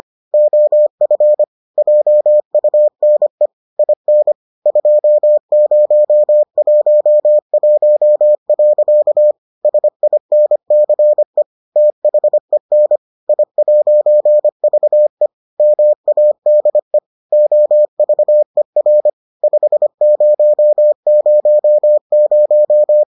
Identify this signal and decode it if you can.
Morse code, heard as a keyed tone, about 25 wpm, 600 Hz. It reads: OF JUNE IN 2011. SINCE THEN I'VE MADE OVER 5000